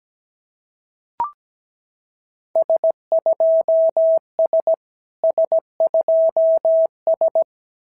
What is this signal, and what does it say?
Morse code, heard as a keyed tone, about 16 wpm, 650 Hz. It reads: S2S S2S
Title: SOTA QSO Practice 17wpm